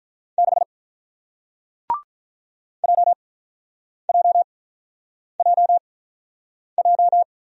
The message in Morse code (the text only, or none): 6 J J J J